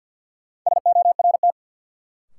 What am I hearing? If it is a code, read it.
Morse code, heard as a keyed tone, about 47 wpm, 700 Hz. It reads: SORT
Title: Sentences from Top 1300 Words - New Word Drill Alt - 50wpm